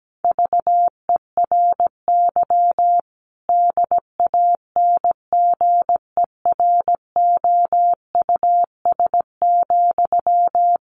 Morse code, heard as a keyed tone, about 17 wpm, 700 Hz. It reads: VERY DANGEROUS,
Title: Sentences from Top 1200 Words 17wpm